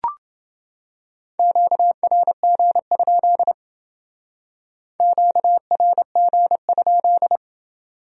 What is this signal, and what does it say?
Morse code, heard as a keyed tone, about 28 wpm, 700 Hz. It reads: QRG? QRG?